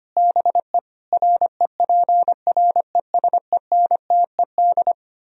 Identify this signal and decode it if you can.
Morse code, heard as a keyed tone, about 31 wpm, 700 Hz. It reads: BE REPRESENTED
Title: Speed-Racing - Sets of 5 Words 25wpm